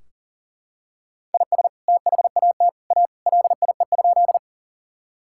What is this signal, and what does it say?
Morse code, heard as a keyed tone, about 40 wpm, 700 Hz. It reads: IS THAT A LIE?
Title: Sentences from Top 1600 Words 40wpm